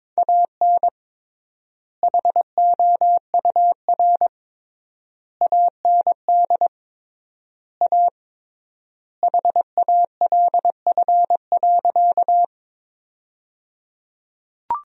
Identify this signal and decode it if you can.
Morse code, heard as a keyed tone, about 21 wpm, 700 Hz. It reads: AN HOUR AND A HALF.
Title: 3x Word-Spacing - Sentences from Top 400 Words - 22wpm